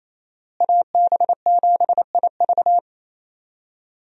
Morse code, heard as a keyed tone, about 28 wpm, 700 Hz. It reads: AB7IV
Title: USA Call Signs - 5 Characters - Repeated 3x - 28wpm